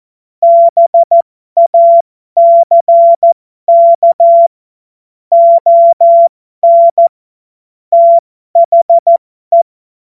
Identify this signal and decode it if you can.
Morse code, heard as a keyed tone, about 14 wpm, spaced as for 12 wpm, 680 Hz. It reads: BACK ON THE